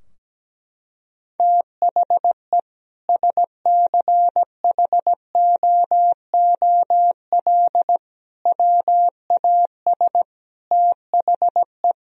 Morse code, heard as a keyed tone, about 17 wpm, 700 Hz. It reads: THE SCHOOL WAS THE
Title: Sentences from Top 1000 Words - Review Part 2 - 17wpm